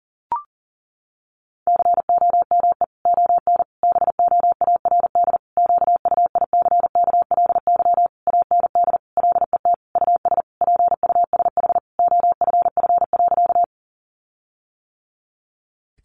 Morse code, heard as a keyed tone, about 40 wpm, 700 Hz. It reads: COME ON BOARD QUICKLY AND LET US PUSH OFF.
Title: Sentences from Top 1300 Words 40wpm